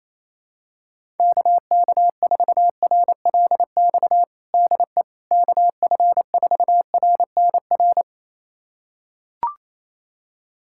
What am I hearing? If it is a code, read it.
Morse code, heard as a keyed tone, about 28 wpm, 700 Hz. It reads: KK4RLX DE KF4RNR
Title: USA Callsign DE Callsign - 6 Characters - 28wpm